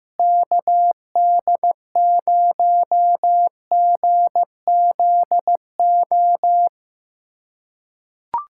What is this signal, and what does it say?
Morse code, heard as a keyed tone, about 15 wpm, 700 Hz. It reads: KD0GZO
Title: USA Callsign DE Callsign 15wpm